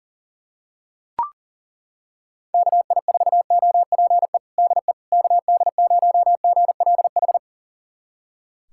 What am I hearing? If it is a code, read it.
Morse code, heard as a keyed tone, about 40 wpm, 700 Hz. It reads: KI4OPE DE KD0GLH
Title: USA Callsign DE Callsign 40wpm